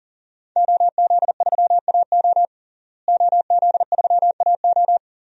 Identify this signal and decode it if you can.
Morse code, heard as a keyed tone, about 40 wpm, 700 Hz. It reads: OZ3AO OZ3AO
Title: DXCC-100 Call Signs - 5 Characters - Repeated 2x - 40wpm